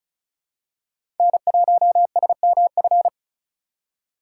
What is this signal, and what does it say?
Morse code, heard as a keyed tone, about 36 wpm, 700 Hz. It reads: N1SMF
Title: USA Call Signs - 4x Character Spacing - 35wpm